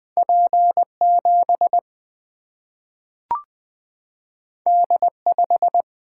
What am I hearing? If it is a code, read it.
Morse code, heard as a keyed tone, about 20 wpm, 700 Hz. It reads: P 7 D 5
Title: Single-Letter Single-Number 20wpm